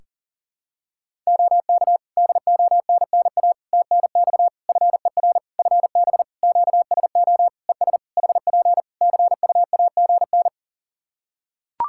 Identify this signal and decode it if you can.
Morse code, heard as a keyed tone, about 40 wpm, 700 Hz. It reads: OK DONNA TNX FER FB QSO ES HP CUAGN
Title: Practice QSO - Exchange 4 - Ending - 40wpm